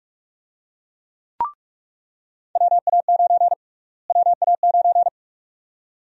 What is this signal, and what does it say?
Morse code, heard as a keyed tone, about 42 wpm, 700 Hz. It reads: WA9 WA9
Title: USA Call Sign Prefixes - Repeated 2x - 45wpm